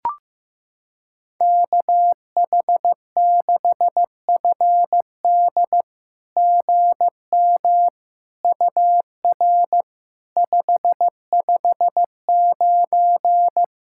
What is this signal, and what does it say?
Morse code, heard as a keyed tone, about 15 wpm, 700 Hz. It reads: KH6FD GM UR 559
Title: SOTA QSO Practice 15wpm